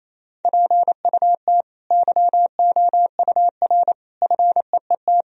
Morse code, heard as a keyed tone, about 28 wpm, 700 Hz. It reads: PUT YOUR FEET
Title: Sentences from Top 1000 Words - Review Part 1 - 28wpm